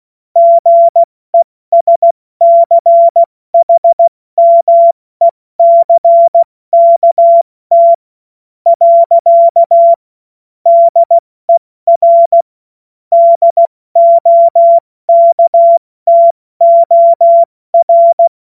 Morse code, heard as a keyed tone, about 16 wpm, 680 Hz. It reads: GESCHMECKT . DER DOKTOR
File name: Morsekurs / Witze / Witz 11. Juli 2023 70 F80